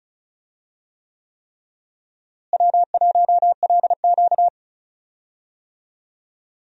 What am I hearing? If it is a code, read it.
Morse code, heard as a keyed tone, about 36 wpm, 700 Hz. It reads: W1LQ
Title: USA Call Signs - 4 Characters - Repeated 3x - 35wpm